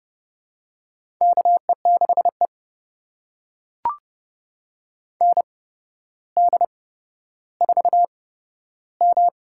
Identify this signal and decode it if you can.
Morse code, heard as a keyed tone, about 30 wpm, spaced as for 13 wpm, 700 Hz. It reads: KE6E ND4M
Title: USA Call Signs - 4 Characters - 8x Character Spacing - 30wpm